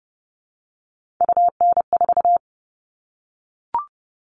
Morse code, heard as a keyed tone, about 30 wpm, 700 Hz. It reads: UN4
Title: All Cut Numbers - 3-Digit 30wpm